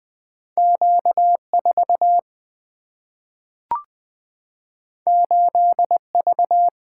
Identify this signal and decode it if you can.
Morse code, heard as a keyed tone, about 20 wpm, 700 Hz. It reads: Q 4 8 V
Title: Two Letter-Number Combinations - 20wpm